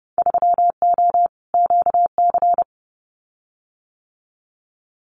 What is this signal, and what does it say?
Morse code, heard as a keyed tone, about 30 wpm, 700 Hz. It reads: 3O QC
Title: Winter Field Day Exchanges Encore 30wpm